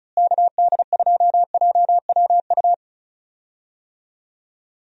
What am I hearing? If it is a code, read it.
Morse code, heard as a keyed tone, about 36 wpm, 700 Hz. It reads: KD2JWU
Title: USA Call Signs - 6 Characters - Repeated 3x - 35wpm